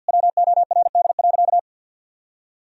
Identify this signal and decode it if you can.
Morse code, heard as a keyed tone, about 51 wpm, 700 Hz. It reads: WORD.
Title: Sentences from Top 1000 Words - Review Part 4 - 50wpm